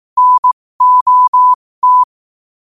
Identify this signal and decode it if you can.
Morse code, heard as a keyed tone, about 18 wpm, spaced as for 15 wpm, 1000 Hz. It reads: NOT
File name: qotdfiles / 15wpm